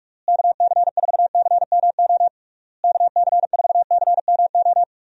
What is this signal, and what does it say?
Morse code, heard as a keyed tone, about 44 wpm, 700 Hz. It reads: KC4CMO KC4CMO
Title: USA Call Signs - 6 Characters - Repeated 3x - 45wpm